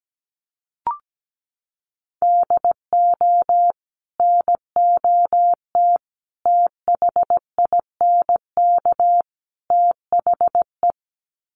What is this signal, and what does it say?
Morse code, heard as a keyed tone, about 17 wpm, 700 Hz. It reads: DO NOT THINK THE
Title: Sets of 4 Words 17wpm